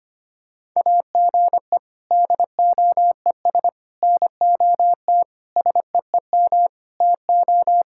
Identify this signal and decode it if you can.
Morse code, heard as a keyed tone, about 25 wpm, 700 Hz. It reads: AGE DOES NOT SEEM TO
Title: Sentences from Top 1000 Words - Review Part 6 - 25wpm